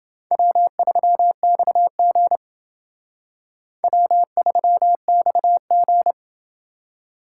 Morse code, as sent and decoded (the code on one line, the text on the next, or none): .-- ...-- -..- --. / .-- ...-- -..- --.
W3XG W3XG